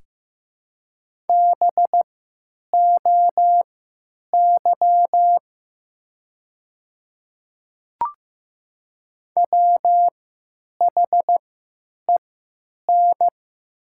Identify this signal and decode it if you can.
Morse code, heard as a keyed tone, about 15 wpm, spaced as for 8 wpm, 700 Hz. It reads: BOY WHEN
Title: 3x Character Spacing - Top 300 Words - 15wpm